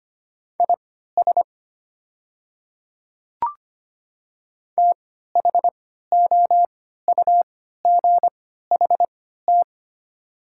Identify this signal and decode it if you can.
Morse code, heard as a keyed tone, about 25 wpm, spaced as for 14 wpm, 700 Hz. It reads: IS THOUGHT
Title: 3x Character Spacing - Top 500 Words - 25wpm